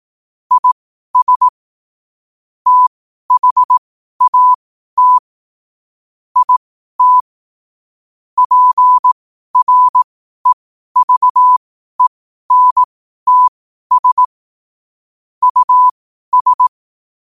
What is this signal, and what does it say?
Morse code, heard as a keyed tone, about 18 wpm, spaced as for 12 wpm, 1000 Hz. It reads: IS THAT IT PREVENTS US
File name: qotdfiles / 12wpm